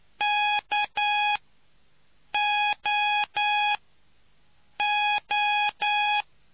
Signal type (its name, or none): Morse code